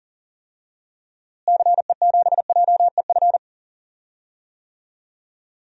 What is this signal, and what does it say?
Morse code, heard as a keyed tone, about 40 wpm, 700 Hz. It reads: CE7JEF
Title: DXCC-100 Call Signs - 6 Characters - Repeated 2x - 40wpm